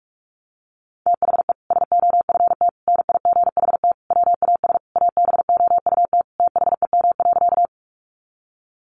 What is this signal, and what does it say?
Morse code, heard as a keyed tone, about 44 wpm, 700 Hz. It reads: THE SOFT NIGHT WAS ABOUT THEM.
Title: Sentences from Top 1500 Words 45wpm